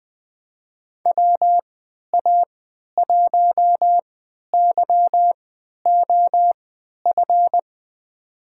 Morse code, heard as a keyed tone, about 20 wpm, spaced as for 11 wpm, 700 Hz. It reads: WA1YOF
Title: USA Call Signs - 6 Characters - 3x Character Spacing - 20wpm